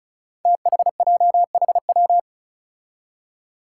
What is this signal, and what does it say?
Morse code, heard as a keyed tone, about 36 wpm, 700 Hz. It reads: THJHW
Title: Five Challenging Letter-Number Combinations 35wpm